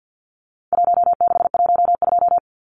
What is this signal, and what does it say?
Morse code, heard as a keyed tone, about 50 wpm, 700 Hz. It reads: 1612
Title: Numbers - Contest Style - 4-Digit 50wpm